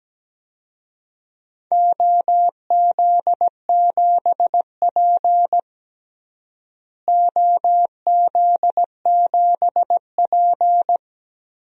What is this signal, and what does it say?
Morse code, heard as a keyed tone, about 17 wpm, 700 Hz. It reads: OZ7P OZ7P
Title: DXCC-100 Call Signs - 4 Characters - Repeated 2x - 17wpm